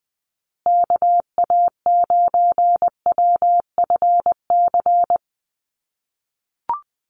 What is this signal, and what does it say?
Morse code, heard as a keyed tone, about 20 wpm, 700 Hz. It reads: KA9WFC
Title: USA Call Signs Repeated 2x 20wpm